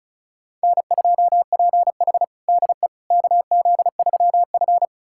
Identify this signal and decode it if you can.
Morse code, heard as a keyed tone, about 34 wpm, 700 Hz. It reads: N2PH DE KZ3F
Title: USA Callsign DE Callsign - 4 Characters - 35wpm